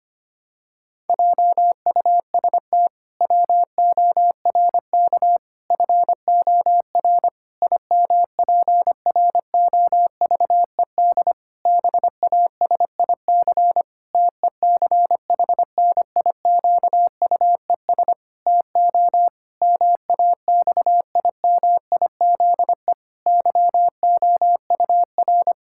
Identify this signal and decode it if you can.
Morse code, heard as a keyed tone, about 25 wpm, 700 Hz. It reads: JUST WORK FOR IMPROVED BASIC TECHNIQUES TO MAXIMIZE YOUR
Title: Pangram Challenge v2 - 25wpm